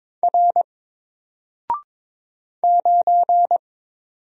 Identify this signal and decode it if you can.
Morse code, heard as a keyed tone, about 22 wpm, 700 Hz. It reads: R 9
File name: Single-Letter-Number-22wpm